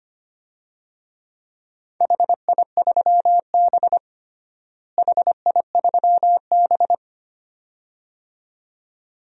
Morse code, heard as a keyed tone, about 25 wpm, 700 Hz. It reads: HI3B HI3B
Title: DXCC-100 Call Signs - 4 Characters - Repeated 2x - 25wpm